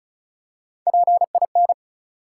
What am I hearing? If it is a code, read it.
Morse code, heard as a keyed tone, about 34 wpm, 700 Hz. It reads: PIN
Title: Three Letter Words - 35wpm